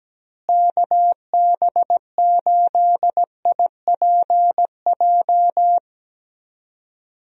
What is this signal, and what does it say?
Morse code, heard as a keyed tone, about 17 wpm, 700 Hz. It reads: KB8IPJ
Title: USA Call Signs - 6 Characters - Repeated 3x - 17wpm